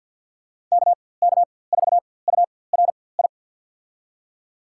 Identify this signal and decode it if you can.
Morse code, heard as a keyed tone, about 47 wpm, spaced as for 23 wpm, 700 Hz. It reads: KK4URI
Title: USA Call Signs - 4x Character Spacing - 50wpm